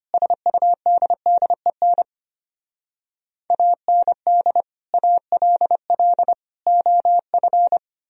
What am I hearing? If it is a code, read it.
Morse code, heard as a keyed tone, about 32 wpm, 700 Hz. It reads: SUDDEN AND ALL OF